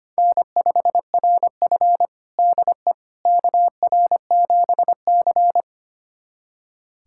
Morse code, heard as a keyed tone, about 25 wpm, 700 Hz. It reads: N5RF DE KR7C
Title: USA Callsign DE Callsign - 4 Characters - 25wpm